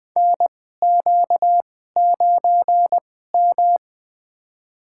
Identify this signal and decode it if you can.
Morse code, heard as a keyed tone, about 20 wpm, spaced as for 14 wpm, 700 Hz. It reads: NQ9M
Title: USA Call Signs - 2x Character Spacing - 20wpm